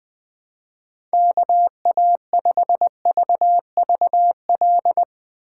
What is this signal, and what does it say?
Morse code, heard as a keyed tone, about 22 wpm, 700 Hz. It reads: KA5VVL
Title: Speed Racing - USA Call Signs 20wpm